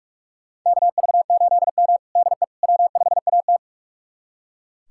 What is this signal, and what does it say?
Morse code, heard as a keyed tone, about 44 wpm, 700 Hz. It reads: KV8M DE W5AT
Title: USA Callsign DE Callsign - 4 Characters - 45wpm